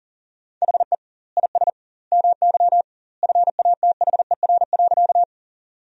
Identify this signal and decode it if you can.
Morse code, heard as a keyed tone, about 40 wpm, 700 Hz. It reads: HE IS MY FATHER.
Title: 2x Word-Spacing - Sentences from Top 300 Words - 40wpm